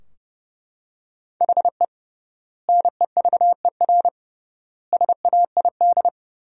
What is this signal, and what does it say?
Morse code, heard as a keyed tone, about 32 wpm, 700 Hz. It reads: HE NEVER SAID
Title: 3x Word-Spacing - Sentences from Top 500 Words - 30wpm